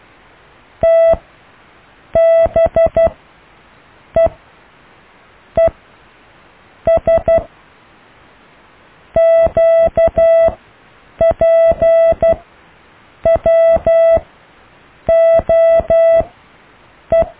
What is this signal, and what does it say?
Morse code, heard as a keyed tone, about 12 wpm, spaced as for 6 wpm, 660 Hz. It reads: TBEES QPWOE
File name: cw / Alle letters